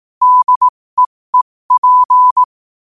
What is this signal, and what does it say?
Morse code, heard as a keyed tone, about 18 wpm, spaced as for 15 wpm, 1000 Hz. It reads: DEEP
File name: qotdfiles / 15wpm